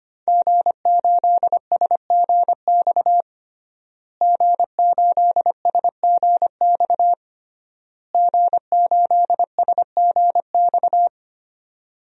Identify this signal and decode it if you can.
Morse code, heard as a keyed tone, about 25 wpm, 700 Hz. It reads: G8SGX G8SGX G8SGX
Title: DXCC-100 Call Signs - 5 Characters - Repeated 3x - 25wpm